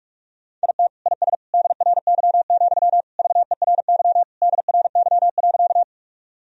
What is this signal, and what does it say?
Morse code, heard as a keyed tone, about 41 wpm, 700 Hz. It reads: IT IS DRY, VERY DRY.
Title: Sentences from Top 1000 Words - Review Part 3 - 45wpm